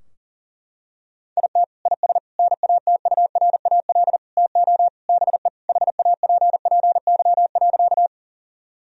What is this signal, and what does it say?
Morse code, heard as a keyed tone, about 40 wpm, 700 Hz. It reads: IT IS NATURAL TO BE HAPPY.
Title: Sentences from Top 1000 Words - Review Part 4 - 40wpm